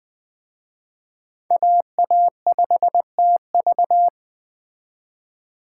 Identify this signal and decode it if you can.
Morse code, heard as a keyed tone, about 20 wpm, 700 Hz. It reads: AA5TV
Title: Speed Racing - USA 2x2 Call Signs 17wpm